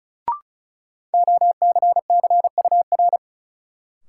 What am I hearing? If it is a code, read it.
Morse code, heard as a keyed tone, about 34 wpm, 700 Hz. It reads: OCCUR
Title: Sentences from Top 900 Words - New Word Drill - 35wpm